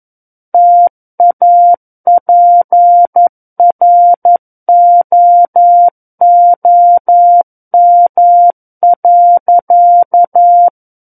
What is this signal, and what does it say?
Morse code, heard as a keyed tone, about 11 wpm, 700 Hz. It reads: TAPROOM.
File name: Book One / 02-Chapter 2 - 11 WPM